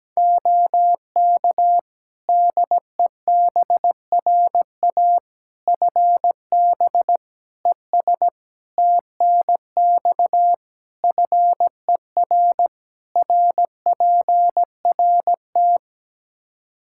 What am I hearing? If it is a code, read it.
Morse code, heard as a keyed tone, about 17 wpm, 700 Hz. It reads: OK DEBRA FB ES TNX FER RPRT
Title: Practice QSO - Exchange 2 - Info - 17wpm